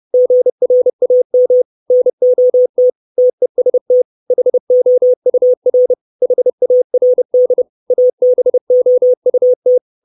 Morse code, heard as a keyed tone, about 30 wpm, 500 Hz. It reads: GRAM NOT TEST HOUR HARD ABOUT
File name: 632 Words 30WPM